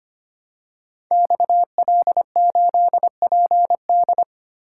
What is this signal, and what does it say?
Morse code, heard as a keyed tone, about 25 wpm, 700 Hz. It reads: XL8PD
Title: Code Groups - 5 Letter:Numbers - 25wpm